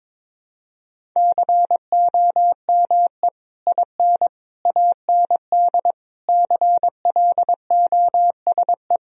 Morse code, heard as a keyed tone, about 22 wpm, 700 Hz. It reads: COME IN AND CLOSE
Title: Sentences from Top 1000 Words - Review Part 1 - 22wpm